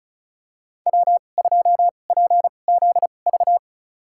Morse code, heard as a keyed tone, about 36 wpm, spaced as for 25 wpm, 700 Hz. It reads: W2PZV
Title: USA Call Signs - 5 Characters - 2x Character Spacing - 35wpm